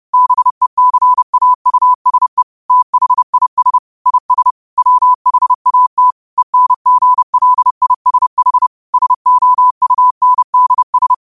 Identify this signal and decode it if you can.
Morse code, heard as a keyed tone, about 30 wpm, 1000 Hz. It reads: BECAUSE THIS IS WHAT ENGLISH SOUNDS